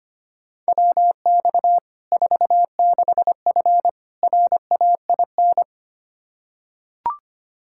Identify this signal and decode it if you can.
Morse code, heard as a keyed tone, about 25 wpm, 700 Hz. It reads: WX 46F RAIN
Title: Practice QSO - Exchange 2 - Info - 25wpm